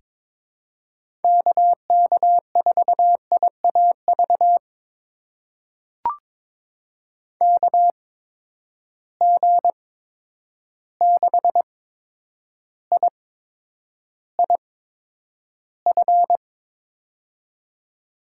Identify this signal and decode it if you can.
Morse code, heard as a keyed tone, about 21 wpm, spaced as for 9 wpm, 700 Hz. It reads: KK4IAV KG6IIF
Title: USA Call Signs - 6 Characters - 8x Character Spacing - 22wpm